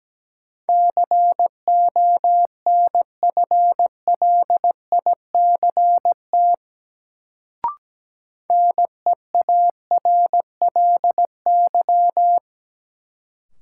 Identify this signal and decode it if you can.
Morse code, heard as a keyed tone, about 17 wpm, 700 Hz. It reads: CONFLICT NEARLY
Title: Sentences from Top 1300 Words - New Word Drill - 17wpm